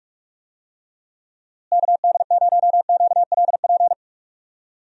Morse code, heard as a keyed tone, about 42 wpm, 700 Hz. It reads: KD0QLP
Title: USA Call Signs - 6 Characters - Repeated 3x - 45wpm